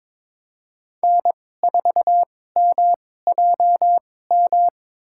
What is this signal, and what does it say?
Morse code, heard as a keyed tone, about 22 wpm, spaced as for 16 wpm, 700 Hz. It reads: N4MJM
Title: USA Call Signs - 2x Character Spacing - 22wpm